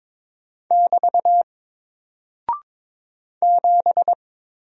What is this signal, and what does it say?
Morse code, heard as a keyed tone, about 22 wpm, 700 Hz. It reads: <BT> 7
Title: Single Character (Letter:Number:Punctuation:Prosign) 22wpm